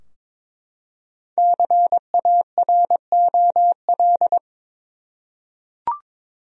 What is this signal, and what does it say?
Morse code, heard as a keyed tone, about 22 wpm, 700 Hz. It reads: CAROL
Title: USA Common Names - Max 5 Letters - 22wpm